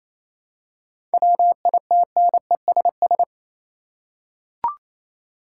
Morse code, heard as a keyed tone, about 28 wpm, 700 Hz. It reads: WITNESS
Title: Seven Letter Words - 28wpm